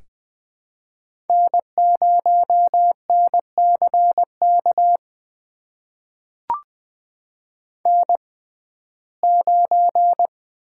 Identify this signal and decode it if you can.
Morse code, heard as a keyed tone, about 20 wpm, spaced as for 15 wpm, 700 Hz. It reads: N0NCK N9
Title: USA Call Signs - 5 Characters - 6x Character Spacing - 20wpm